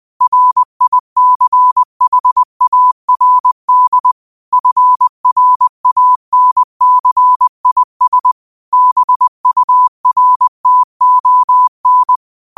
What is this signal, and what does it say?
Morse code, heard as a keyed tone, about 20 wpm, 1000 Hz. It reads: RICHARD FRANCIS BURTON
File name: qotdfiles / 20wpm